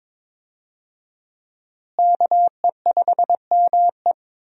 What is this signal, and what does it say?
Morse code, heard as a keyed tone, about 21 wpm, 700 Hz. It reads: KE5ME
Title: USA Call Signs - 5 Characters - Repeated 3x - 22wpm